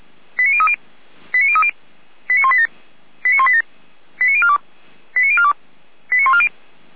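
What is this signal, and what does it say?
Five-tone selective call (ZVEI), heard as quick runs of five tones, tones of about 70 ms. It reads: 89030 89030 89187 89187 89042 89042 89140
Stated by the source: ZVEI five-tone selective call received by SDR.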